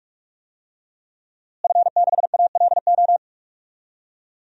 Morse code, heard as a keyed tone, about 41 wpm, 700 Hz. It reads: F6ALO
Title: DXCC-100 Call Signs - 5 Characters - Repeated 3x - 45wpm